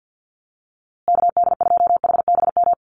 Morse code, heard as a keyed tone, about 50 wpm, 700 Hz. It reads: KD2HBM
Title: USA Call Signs Repeated 3x 50wpm